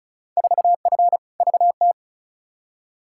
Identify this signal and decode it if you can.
Morse code, heard as a keyed tone, about 36 wpm, 700 Hz. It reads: 4F VT
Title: Field Day Exchanges Encore - 35wpm